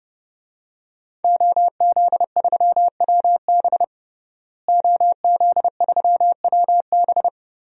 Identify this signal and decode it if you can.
Morse code, heard as a keyed tone, about 30 wpm, 700 Hz. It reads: OZ3WB OZ3WB
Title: DXCC-100 Call Signs - 5 Characters - Repeated 3x - 30wpm